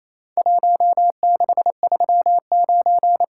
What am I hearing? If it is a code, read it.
Morse code, heard as a keyed tone, about 27 wpm, 700 Hz. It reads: 1639
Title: Worked All Europe DX Contest - QTC Exchange - 28wpm